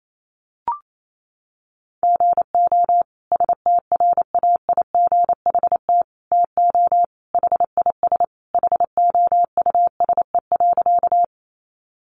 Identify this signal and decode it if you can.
Morse code, heard as a keyed tone, about 28 wpm, 700 Hz. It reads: GO STRAIGHT TO HIS HOUSE.
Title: Sentences from Top 1000 Words 28wpm